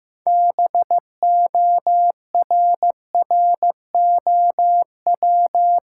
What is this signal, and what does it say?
Morse code, heard as a keyed tone, about 15 wpm, 700 Hz. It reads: BORROW
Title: Sentences from Top 1500 Words - New Word Drill Alt - 15wpm